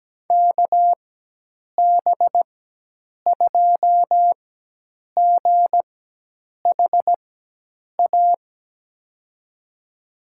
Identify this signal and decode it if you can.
Morse code, heard as a keyed tone, about 17 wpm, spaced as for 8 wpm, 700 Hz. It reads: KB2GHA
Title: USA Call Signs - 4x Character Spacing - 17wpm